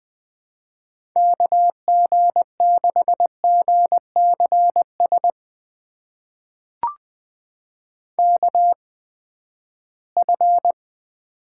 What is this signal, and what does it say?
Morse code, heard as a keyed tone, about 20 wpm, spaced as for 14 wpm, 700 Hz. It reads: KG6GCS KF
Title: USA Call Signs - 6 Characters - 8x Character Spacing - 20wpm